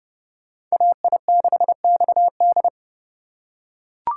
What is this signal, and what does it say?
Morse code, heard as a keyed tone, about 30 wpm, 700 Hz. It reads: AI6XD
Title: USA Call Signs - 5 Characters - 2x Character Spacing - 30wpm